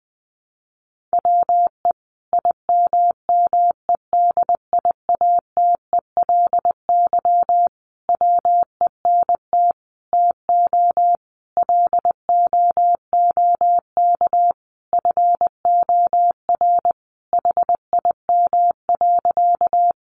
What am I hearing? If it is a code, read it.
Morse code, heard as a keyed tone, about 20 wpm, 700 Hz. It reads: WE IMMEDIATELY WENT TO LOOK FOR HIM.
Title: Sentences from Top 1100 Words 20wpm